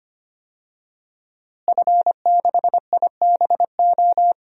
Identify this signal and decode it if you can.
Morse code, heard as a keyed tone, about 25 wpm, 700 Hz. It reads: F6IBO
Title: DXCC-100 Call Signs - 5 Characters - Repeated 2x - 25wpm